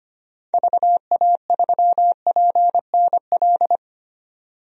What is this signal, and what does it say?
Morse code, heard as a keyed tone, about 25 wpm, 700 Hz. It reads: VA3PNL
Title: DXCC-100 Call Signs - 6 Characters - 25wpm